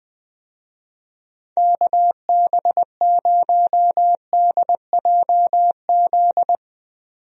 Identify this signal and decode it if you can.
Morse code, heard as a keyed tone, about 20 wpm, 700 Hz. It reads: KB0DJZ
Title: USA Call Signs - 6 Characters - 20wpm